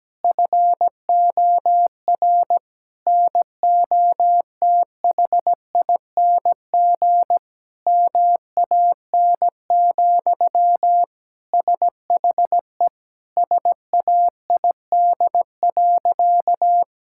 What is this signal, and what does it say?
Morse code, heard as a keyed tone, about 17 wpm, 700 Hz. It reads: FOR NOTHING MAN, SHE SAID.
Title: Sentences from Top 1000 Words - Review Part 6 - 17wpm